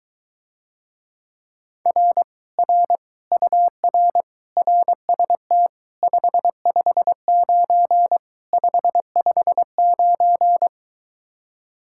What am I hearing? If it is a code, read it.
Morse code, heard as a keyed tone, about 23 wpm, 700 Hz. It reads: R R UR RST 559 559